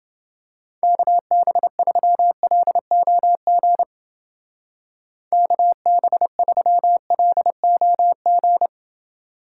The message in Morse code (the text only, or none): KB3LOG KB3LOG